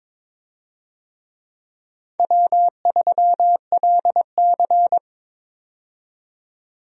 Morse code, heard as a keyed tone, about 22 wpm, 700 Hz. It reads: W3LC
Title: USA Call Signs - 4 Characters - 22wpm